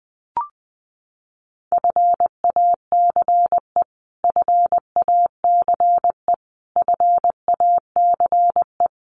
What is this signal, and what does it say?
Morse code, heard as a keyed tone, about 20 wpm, 700 Hz. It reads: FACE FACE FACE
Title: Top 500 Words Repeated 20wpm